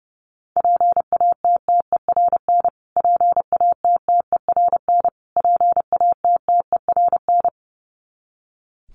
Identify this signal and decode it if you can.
Morse code, heard as a keyed tone, about 30 wpm, 700 Hz. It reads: PATTERN PATTERN PATTERN
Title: Top 500 Words Repeated 30wpm